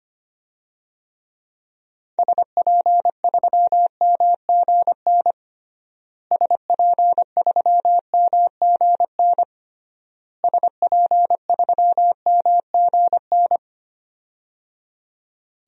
Morse code, heard as a keyed tone, about 25 wpm, 700 Hz. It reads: SP3MGN SP3MGN SP3MGN
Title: DXCC-100 Call Signs - 6 Characters - Repeated 3x - 25wpm